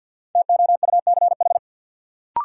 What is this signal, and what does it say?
Morse code, heard as a keyed tone, about 51 wpm, 700 Hz. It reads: TOUCH
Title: Sentences from Top 1200 Words - New Word Drill Alt - 50wpm